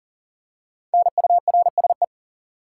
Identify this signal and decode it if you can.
Morse code, heard as a keyed tone, about 40 wpm, 700 Hz. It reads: NURSE
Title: Practice QSO - Exchange 3 - More Info - 40wpm